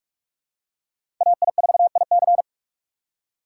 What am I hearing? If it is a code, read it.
Morse code, heard as a keyed tone, about 46 wpm, 700 Hz. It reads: AI4IC
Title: USA Call Signs - 5 Characters - 3x Character Spacing - 45wpm